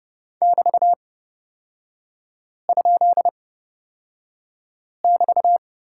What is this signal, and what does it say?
Morse code, heard as a keyed tone, about 30 wpm, spaced as for 5 wpm, 700 Hz. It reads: <BT>?<BT>
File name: Lesson 36e - BT - Three Characters 30wpm